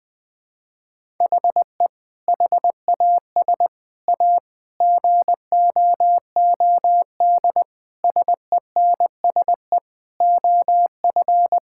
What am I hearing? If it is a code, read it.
Morse code, heard as a keyed tone, about 20 wpm, 700 Hz. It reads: HE HAS A GOOD SENSE OF
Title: Sentences with 2-Word Phrases - Spoken Beginning - 20wpm